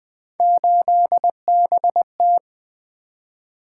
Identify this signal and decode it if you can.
Morse code, heard as a keyed tone, about 20 wpm, 700 Hz. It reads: 8BT
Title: Three Letter-Number Combinations - 20wpm